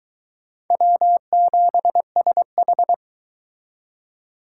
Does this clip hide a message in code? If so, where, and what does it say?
Yes: Morse code from 0.7 to 2.9 s, heard as a keyed tone, about 22 wpm, 700 Hz. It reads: W7SH